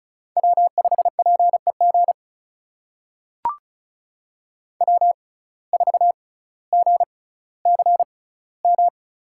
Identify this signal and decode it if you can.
Morse code, heard as a keyed tone, about 36 wpm, spaced as for 18 wpm, 700 Hz. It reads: W5PEG W4GCM
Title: USA Call Signs - 5 Characters - 6x Character Spacing - 35wpm